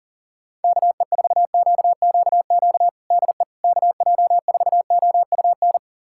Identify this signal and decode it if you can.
Morse code, heard as a keyed tone, about 40 wpm, 700 Hz. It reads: KE4QQQ DE KJ4OUN
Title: USA Callsign DE Callsign - 6 Characters - 40wpm